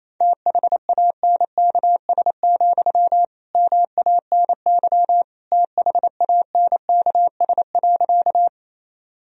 Morse code, heard as a keyed tone, about 28 wpm, 700 Hz. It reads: THANKS, MANY THANKS.
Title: Sentences from Top 1000 Words - Review Part 3 - 28wpm